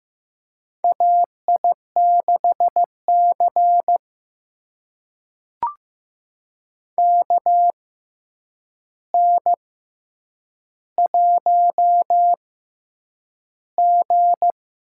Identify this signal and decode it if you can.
Morse code, heard as a keyed tone, about 15 wpm, spaced as for 8 wpm, 700 Hz. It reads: AI6C KN1G
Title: USA Call Signs - 4 Characters - 6x Character Spacing - 15wpm